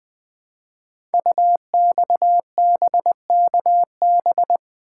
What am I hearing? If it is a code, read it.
Morse code, heard as a keyed tone, about 20 wpm, 700 Hz. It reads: UXBKB
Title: Code Groups - 5 Letters - 20wpm